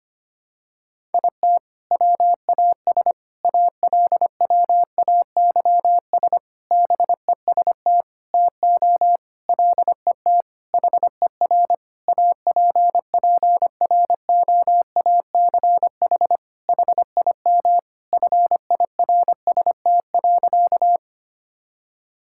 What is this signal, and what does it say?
Morse code, heard as a keyed tone, about 25 wpm, 700 Hz. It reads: IT WAS ALWAYS BEST TO LET HER APPROACH HIM FIRST.
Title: Sentences from Top 1000 Words - Review Part 6 - 25wpm